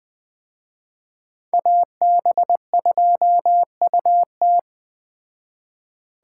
Morse code, heard as a keyed tone, about 20 wpm, 700 Hz. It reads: AB2UT
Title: Speed Racing - USA 2x2 Call Signs 17wpm